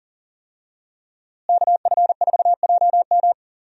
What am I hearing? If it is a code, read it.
Morse code, heard as a keyed tone, about 40 wpm, 700 Hz. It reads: KF4JM
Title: USA Call Signs - 5 Characters - Repeated 3x - 40wpm